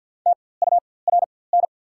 Morse code, heard as a keyed tone, about 54 wpm, spaced as for 24 wpm, 690 Hz. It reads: TURN
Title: 4x Character Spacing - Top 500 Words - 50wpm